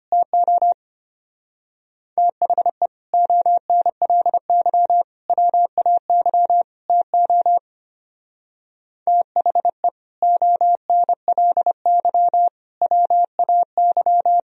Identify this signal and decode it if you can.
Morse code, heard as a keyed tone, about 32 wpm, 700 Hz. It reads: TO THE ONLY WAY TO THE ONLY WAY